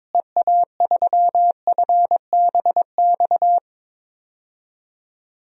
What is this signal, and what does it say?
Morse code, heard as a keyed tone, about 22 wpm, 700 Hz. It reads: EA3FBX
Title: DXCC-100 Call Signs - 6 Characters - Repeated 2x - 22wpm